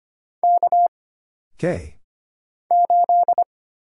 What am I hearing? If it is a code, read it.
Morse code, heard as a keyed tone, about 25 wpm, 700 Hz. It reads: K 8
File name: Single Character (Letter:Number:Punctuation:Prosign) - Mind-Melt - 25wpm